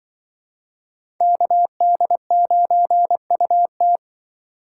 Morse code, heard as a keyed tone, about 25 wpm, 700 Hz. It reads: KD9UT
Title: Speed Racing - USA 2x2 Call Signs 20wpm